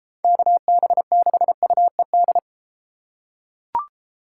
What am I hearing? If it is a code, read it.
Morse code, heard as a keyed tone, about 32 wpm, 700 Hz. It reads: KB6UED